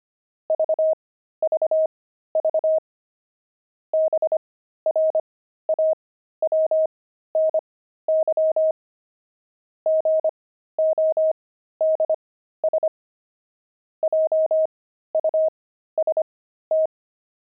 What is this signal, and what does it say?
Morse code, heard as a keyed tone, about 25 wpm, spaced as for 13 wpm, 630 Hz. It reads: VVV BRAWNY GODS JUST